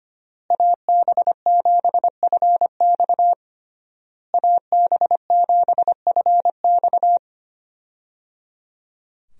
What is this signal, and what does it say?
Morse code, heard as a keyed tone, about 25 wpm, 700 Hz. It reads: AB7FX AB7FX
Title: USA Call Signs - 5 Characters - Repeated 3x - 25wpm